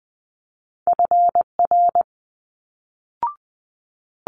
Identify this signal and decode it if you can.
Morse code, heard as a keyed tone, about 20 wpm, 700 Hz. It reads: FR
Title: Multi-letter Phonemes Quick Fire 20wpm